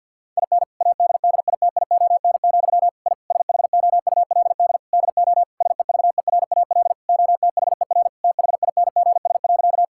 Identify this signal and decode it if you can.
Morse code, heard as a keyed tone, about 50 wpm, 700 Hz. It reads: IN ADDITION, I SHOULD DO SEVERAL OTHER THINGS.
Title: Sentences from Top 1000 Words - Review Part 6 - 50wpm